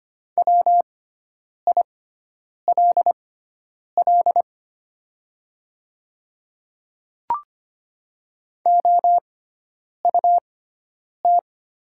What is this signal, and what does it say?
Morse code, heard as a keyed tone, about 25 wpm, spaced as for 9 wpm, 700 Hz. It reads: WILL OUT
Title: 6x Character Spacing - Top 100 Words - 25wpm